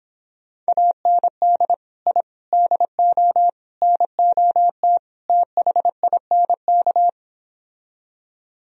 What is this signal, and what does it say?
Morse code, heard as a keyed tone, about 26 wpm, 700 Hz. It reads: AND I DO NOT THINK